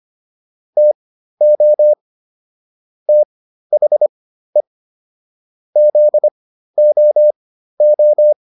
Morse code, heard as a keyed tone, about 25 wpm, spaced as for 13 wpm, 600 Hz. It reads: TO THE ZOO